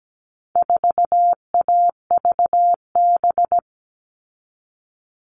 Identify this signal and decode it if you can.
Morse code, heard as a keyed tone, about 17 wpm, 700 Hz. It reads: 4AVB
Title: All Cut Numbers - 4-Digit 17wpm